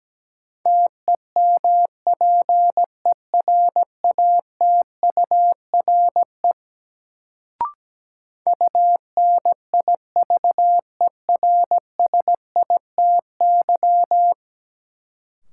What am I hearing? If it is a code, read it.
Morse code, heard as a keyed tone, about 17 wpm, 700 Hz. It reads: TEMPERATURE UNIVERSITY
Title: Sentences from Top 1000 Words - Review Part 4 - New Word Drill - 17wpm